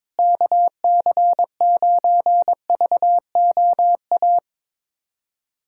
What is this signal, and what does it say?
Morse code, heard as a keyed tone, about 22 wpm, 700 Hz. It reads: KC9VOA
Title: USA Callsign DE Callsign - 6 Characters - 22wpm